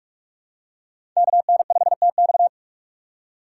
Morse code, heard as a keyed tone, about 46 wpm, 700 Hz. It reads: KN5TX
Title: K1USN SST QSO Practice - 45wpm